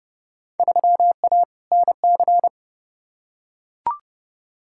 Morse code, heard as a keyed tone, about 30 wpm, 700 Hz. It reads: 3A NC
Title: Field Day Exchanges - 30wpm